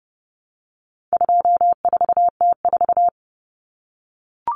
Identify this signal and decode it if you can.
Morse code, heard as a keyed tone, about 30 wpm, 700 Hz. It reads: 24T4
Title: Common Cut Numbers - 4-Digit 30wpm